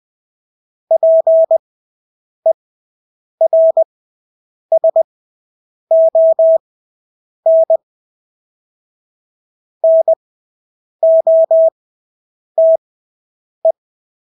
Morse code, heard as a keyed tone, about 20 wpm, spaced as for 8 wpm, 650 Hz. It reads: PERSON NOTE
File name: Engelske ord 20/8 wpm